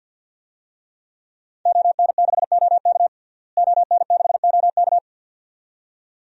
Morse code, heard as a keyed tone, about 52 wpm, 700 Hz. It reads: ON6OK ON6OK
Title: DXCC-100 Call Signs - 5 Characters - Repeated 2x - 50wpm